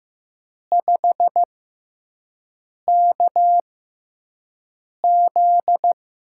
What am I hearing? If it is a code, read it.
Morse code, heard as a keyed tone, about 15 wpm, spaced as for 5 wpm, 700 Hz. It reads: 5KZ
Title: USA Call Signs - 5 Characters - 6x Character Spacing - 15wpm